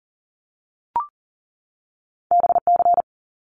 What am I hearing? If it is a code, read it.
Morse code, heard as a keyed tone, about 41 wpm, 700 Hz. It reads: BC
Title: Canadian Provinces and Territories 40wpm